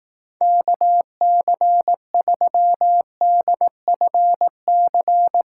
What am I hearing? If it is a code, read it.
Morse code, heard as a keyed tone, about 18 wpm, 700 Hz. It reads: KC3DFC
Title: Speed Racing - USA 2x3 Call Signs 15wpm